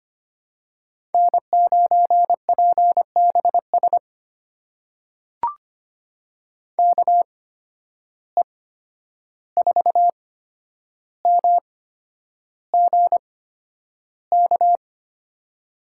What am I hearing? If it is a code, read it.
Morse code, heard as a keyed tone, about 25 wpm, spaced as for 10 wpm, 700 Hz. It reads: N9PBS KE4MGK
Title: USA Call Signs - 8x Character Spacing - 25wpm